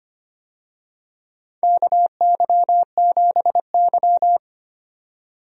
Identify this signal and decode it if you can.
Morse code, heard as a keyed tone, about 25 wpm, 700 Hz. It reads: KY7Y
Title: USA Call Signs - 4 Characters - 25wpm